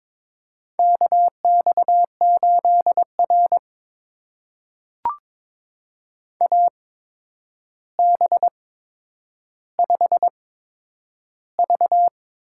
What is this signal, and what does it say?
Morse code, heard as a keyed tone, about 21 wpm, spaced as for 9 wpm, 700 Hz. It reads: KX8R AB5V
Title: USA Call Signs - 4 Characters - 8x Character Spacing - 22wpm